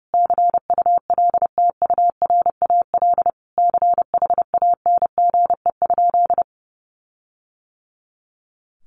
Morse code, heard as a keyed tone, about 30 wpm, 700 Hz. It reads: CULTURAL CHANGE?
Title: Sentences from Top 1500 Words 30wpm